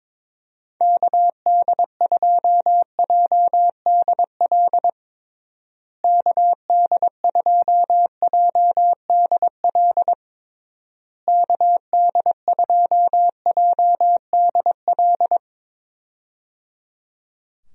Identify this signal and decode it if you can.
Morse code, heard as a keyed tone, about 22 wpm, 700 Hz. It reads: KD2JDL KD2JDL KD2JDL
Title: USA Call Signs - 6 Characters - Repeated 3x - 22wpm